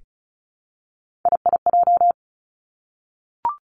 Morse code, heard as a keyed tone, about 36 wpm, 700 Hz. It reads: IIJ
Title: Three Challenging Letter-Number Combinations 35wpm